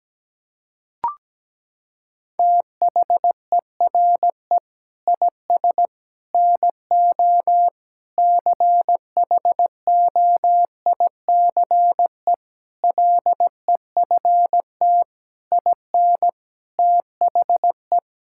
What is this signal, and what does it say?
Morse code, heard as a keyed tone, about 17 wpm, 700 Hz. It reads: THERE IS NO CHOICE LEFT IN THE
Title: Sentences from Top 1000 Words - Review Part 6 - 17wpm